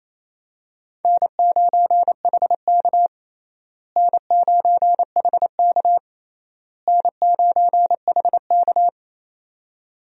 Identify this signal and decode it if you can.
Morse code, heard as a keyed tone, about 28 wpm, 700 Hz. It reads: N9HK N9HK N9HK
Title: USA Call Signs - 4 Characters - Repeated 3x - 28wpm